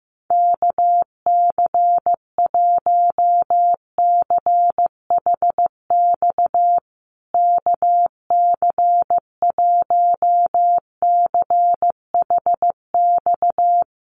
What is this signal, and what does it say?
Morse code, heard as a keyed tone, about 15 wpm, 700 Hz. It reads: KC1CHX KC1CHX
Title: USA Call Signs Repeated 3x 15wpm